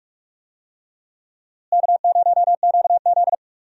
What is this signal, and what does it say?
Morse code, heard as a keyed tone, about 46 wpm, 700 Hz. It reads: K0QZ
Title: USA Call Signs - 4 Characters - Repeated 3x - 45wpm